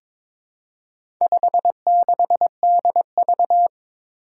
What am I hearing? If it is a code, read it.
Morse code, heard as a keyed tone, about 22 wpm, 700 Hz. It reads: 56DV
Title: Four Challenging Letter-Number Combinations 22wpm